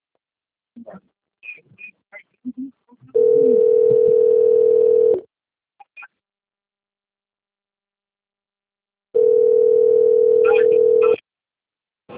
Ringback tone (North America), 440 + 480 Hz.
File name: out- - - - - .16435